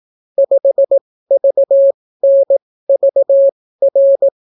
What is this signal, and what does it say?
Morse code, heard as a keyed tone, about 18 wpm, spaced as for 14 wpm, 550 Hz. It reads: 5VNVR